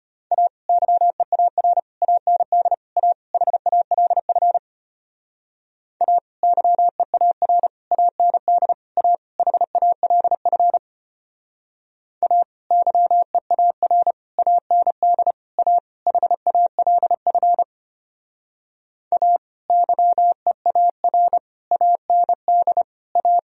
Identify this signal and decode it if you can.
Morse code, heard as a keyed tone, about 32 wpm, 700 Hz. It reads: A YEAR AND A HALF A YEAR AND A HALF A YEAR AND A HALF A YEAR AND A